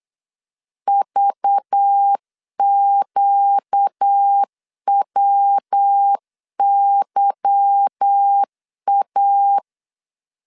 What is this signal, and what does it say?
Morse code, heard as a keyed tone, about 8 wpm, 800 Hz. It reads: VQWYA
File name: CW / DET / 1 9 1500C S C 8wpm 5653